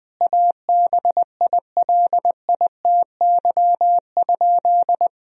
Morse code, heard as a keyed tone, about 20 wpm, 700 Hz. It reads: ABILITY?
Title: Sentences from Top 1000 Words - Review Part 4 - 20wpm